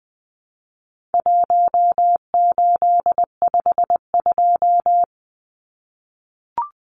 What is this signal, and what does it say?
Morse code, heard as a keyed tone, about 20 wpm, 700 Hz. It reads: 1852
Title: Numbers - 4-Digit 20wpm